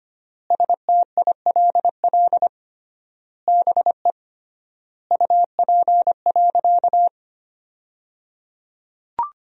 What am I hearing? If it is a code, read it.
Morse code, heard as a keyed tone, about 25 wpm, 700 Hz. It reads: STILL BE UP.
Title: 3x Word-Spacing - Sentences from Top 300 Words - 25wpm